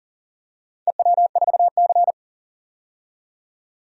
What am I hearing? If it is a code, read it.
Morse code, heard as a keyed tone, about 40 wpm, 700 Hz. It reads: EW4C
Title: DXCC-100 Call Signs - 4 Characters - 40wpm